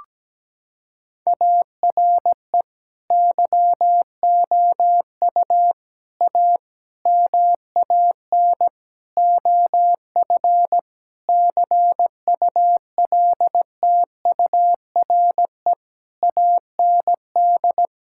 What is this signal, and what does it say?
Morse code, heard as a keyed tone, about 17 wpm, 700 Hz. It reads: ARE YOU A MAN OF CULTURE AND
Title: Sentences from Top 1000 Words - Review Part 6 - 17wpm